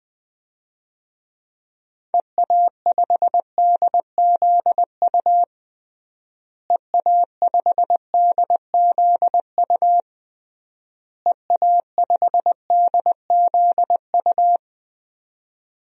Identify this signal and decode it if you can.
Morse code, heard as a keyed tone, about 20 wpm, 700 Hz. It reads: EA5DZU EA5DZU EA5DZU
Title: DXCC-100 Call Signs - 6 Characters - Repeated 3x - 20wpm